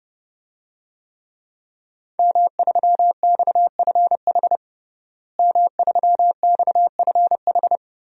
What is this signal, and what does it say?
Morse code, heard as a keyed tone, about 30 wpm, 700 Hz. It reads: M3XFH M3XFH
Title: DXCC-100 Call Signs - 5 Characters - Repeated 2x - 30wpm